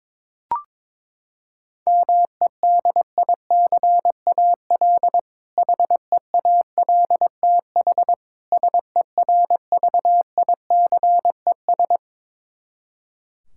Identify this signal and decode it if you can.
Morse code, heard as a keyed tone, about 22 wpm, 700 Hz. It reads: MEDICAL HEALTH SERVICES
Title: K1USN SST QSO Practice 22wpm